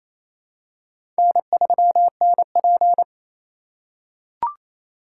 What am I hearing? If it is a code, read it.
Morse code, heard as a keyed tone, about 27 wpm, 700 Hz. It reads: N3NP
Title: USA Call Signs - 4 Characters - 4x Character Spacing - 28wpm